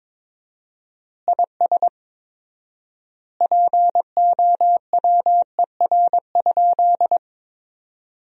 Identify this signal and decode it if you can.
Morse code, heard as a keyed tone, about 21 wpm, 700 Hz. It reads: IS POWER?
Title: 4x Word-Spacing - Sentences from Top 500 Words - 22wpm